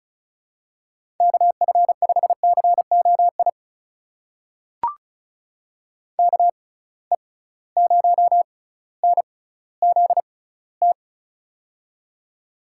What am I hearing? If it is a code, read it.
Morse code, heard as a keyed tone, about 36 wpm, spaced as for 18 wpm, 700 Hz. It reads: KF5COI KE0NZT
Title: USA Call Signs - 6 Characters - 6x Character Spacing - 35wpm